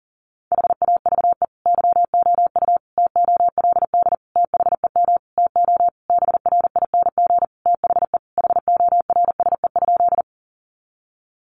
Morse code, heard as a keyed tone, about 40 wpm, 700 Hz. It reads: HAVE YOU TOLD THEM TO BRING THE HORSE?
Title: Sentences from Top 500 Words 40wpm